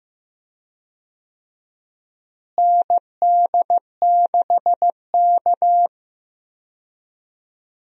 Morse code, heard as a keyed tone, about 15 wpm, 700 Hz. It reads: ND6K
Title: USA Call Signs - 4 Characters - Repeated 3x - 15wpm